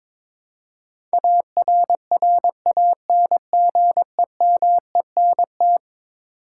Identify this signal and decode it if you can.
Morse code, heard as a keyed tone, about 21 wpm, 700 Hz. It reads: ARRANGEMENT
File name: practice / 22wpm / Eleven Letter Words - 22wpm